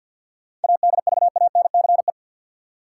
Morse code, heard as a keyed tone, about 47 wpm, 700 Hz. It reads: ADVANCE
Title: Sentences from Top 1400 Words - New Word Drill - 50wpm